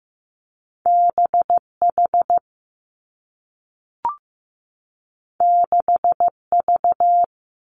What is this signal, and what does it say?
Morse code, heard as a keyed tone, about 15 wpm, 700 Hz. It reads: B H 6 V
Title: Two Challenging Letter-Number Combinations 15wpm